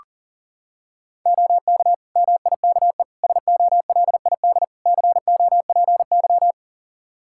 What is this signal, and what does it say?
Morse code, heard as a keyed tone, about 37 wpm, 700 Hz. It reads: OK MIKE SOLID COPY
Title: Practice QSO - Exchange 3 - More Info - 40wpm